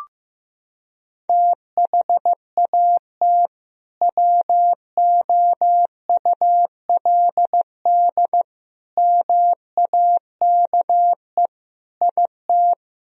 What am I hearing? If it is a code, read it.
Morse code, heard as a keyed tone, about 15 wpm, 700 Hz. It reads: THAT WOULD MAKE IT
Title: Sentences from Top 1000 Words - Review Part 2 - 15wpm